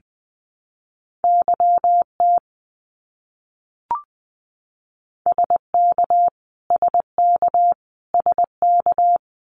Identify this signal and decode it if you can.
Morse code, heard as a keyed tone, about 19 wpm, 700 Hz. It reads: YT SK SK SK
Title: Canadian Provinces and Territories - Repeated 3x - 20wpm